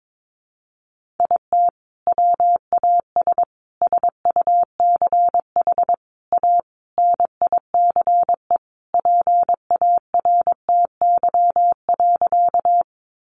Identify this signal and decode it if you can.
Morse code, heard as a keyed tone, about 22 wpm, 700 Hz. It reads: IT WAS SUCH A NICE PARTY.
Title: Sentences from Top 1400 Words 22wpm